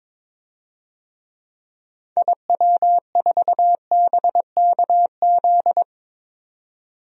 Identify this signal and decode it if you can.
Morse code, heard as a keyed tone, about 22 wpm, 700 Hz. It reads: IW4BKZ
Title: DXCC-100 Call Signs - 6 Characters - 22wpm